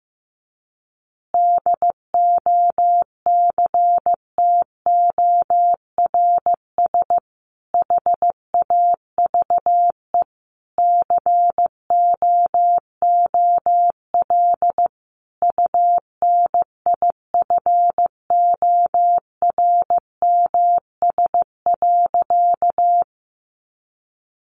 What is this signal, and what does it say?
Morse code, heard as a keyed tone, about 14 wpm, 700 Hz. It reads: DOCTORS HAVE COOL UNIFORMS.
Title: Easy Phrases 15wpm